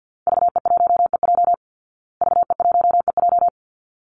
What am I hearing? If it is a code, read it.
Morse code, heard as a keyed tone, about 49 wpm, 700 Hz. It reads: VE1EJ VE1EJ
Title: Call Signs Repeated 3x - 50wpm